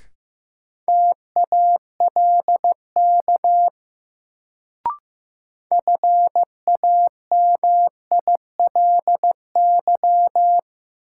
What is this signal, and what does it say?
Morse code, heard as a keyed tone, about 15 wpm, 700 Hz. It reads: TALK FAMILY
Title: Sentences from Top 400 Words - New Word Drill - 15wpm